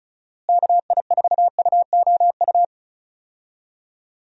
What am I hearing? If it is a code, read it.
Morse code, heard as a keyed tone, about 34 wpm, 700 Hz. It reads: KI4UOU
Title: USA Call Signs - 6 Characters - Repeated 3x - 35wpm